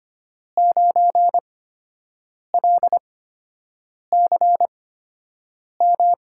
Morse code, heard as a keyed tone, about 25 wpm, spaced as for 7 wpm, 700 Hz. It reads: 9LCM
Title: USA Call Signs - 5 Characters - 8x Character Spacing - 25wpm